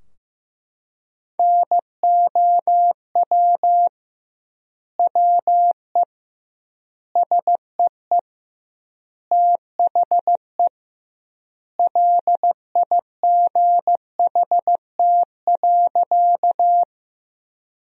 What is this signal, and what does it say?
Morse code, heard as a keyed tone, about 15 wpm, 700 Hz. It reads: NOW WE SEE THE LIGHT.
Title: 2x Word-Spacing - Sentences from Top 200 Words - 15wpm